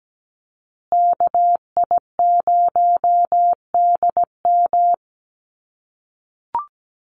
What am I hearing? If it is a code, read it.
Morse code, heard as a keyed tone, about 17 wpm, 700 Hz. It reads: KI0DM
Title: USA Call Signs Repeated 3x 17wpm